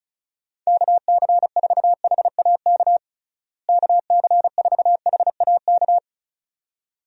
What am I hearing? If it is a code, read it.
Morse code, heard as a keyed tone, about 36 wpm, 700 Hz. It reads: KC4HAK KC4HAK
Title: USA Call Signs - 6 Characters - Repeated 2x - 35wpm